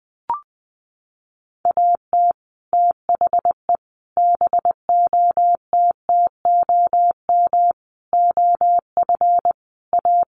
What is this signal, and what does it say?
Morse code, heard as a keyed tone, about 19 wpm, 700 Hz. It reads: AT THE BOTTOM OF A
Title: Sets of 5 Words Encore2 20wpm